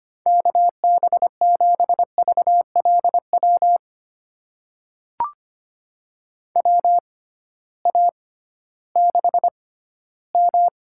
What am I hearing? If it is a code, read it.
Morse code, heard as a keyed tone, about 25 wpm, spaced as for 15 wpm, 700 Hz. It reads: KB7VLW WA6M
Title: USA Call Signs - 6 Characters - 6x Character Spacing - 25wpm